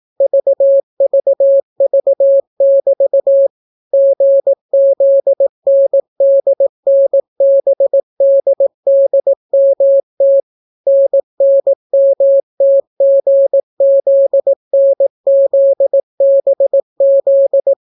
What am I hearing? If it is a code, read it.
Morse code, heard as a keyed tone, about 18 wpm, 550 Hz. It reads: VVV<BT> GZNDNBDDMT NNMTGZNZBZ